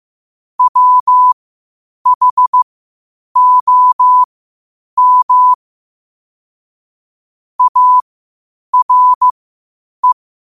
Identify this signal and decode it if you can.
Morse code, heard as a keyed tone, about 15 wpm, spaced as for 8 wpm, 1000 Hz. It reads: WHOM ARE
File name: qotdfiles / 8wpm